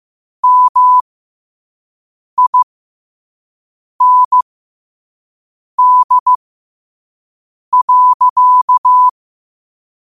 Morse code, heard as a keyed tone, about 15 wpm, spaced as for 5 wpm, 1000 Hz. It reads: MIND.
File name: qotdfiles / 5wpm